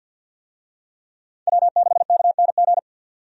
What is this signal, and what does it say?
Morse code, heard as a keyed tone, about 45 wpm, 700 Hz. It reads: W6KNG
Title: USA Call Signs - 5 Characters - 50wpm